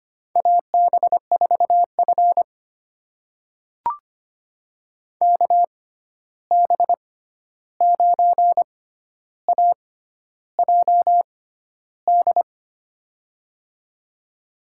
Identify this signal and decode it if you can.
Morse code, heard as a keyed tone, about 25 wpm, spaced as for 11 wpm, 700 Hz. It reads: AB4F KB9AJD
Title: USA Call Signs - 6x Character Spacing - 25wpm